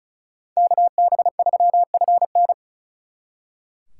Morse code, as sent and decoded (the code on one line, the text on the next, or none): -.- -... ...-- ..-. -.
KB3FN